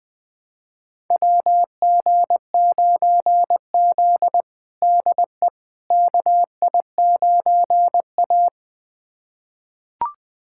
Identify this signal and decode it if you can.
Morse code, heard as a keyed tone, about 20 wpm, 700 Hz. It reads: WG9Z DE KI9A
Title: USA Callsign DE Callsign - 4 Characters - 20wpm